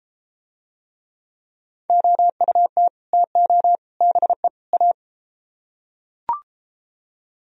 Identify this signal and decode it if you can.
Morse code, heard as a keyed tone, about 33 wpm, 700 Hz. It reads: OUT TO BE A